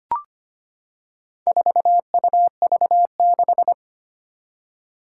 Morse code, heard as a keyed tone, about 25 wpm, 700 Hz. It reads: 4UV6
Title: Four Challenging Letter-Number Combinations 25wpm